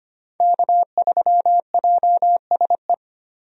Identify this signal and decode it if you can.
Morse code, heard as a keyed tone, about 25 wpm, 700 Hz. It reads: K3JSE
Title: USA Call Signs - 5 Characters - Repeated 3x - 25wpm